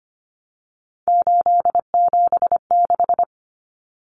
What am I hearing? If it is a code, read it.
Morse code, heard as a keyed tone, about 25 wpm, 700 Hz. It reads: 876
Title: Common Cut Numbers - 3-Digit 25wpm